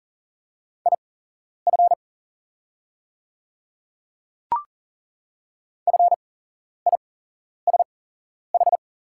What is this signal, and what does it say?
Morse code, heard as a keyed tone, about 40 wpm, spaced as for 11 wpm, 690 Hz. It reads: IF FISH
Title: 8x Character Spacing - Top 300 Words - 40wpm